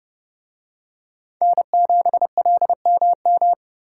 Morse code, heard as a keyed tone, about 26 wpm, 700 Hz. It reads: N7LMM